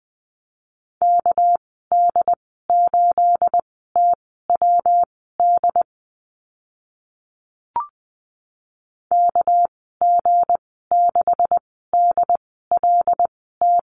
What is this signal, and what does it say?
Morse code, heard as a keyed tone, about 20 wpm, spaced as for 14 wpm, 700 Hz. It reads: KD8TWD KG6DLT
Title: 238 Lesson 38g - Letter q - Call Signs 20wpm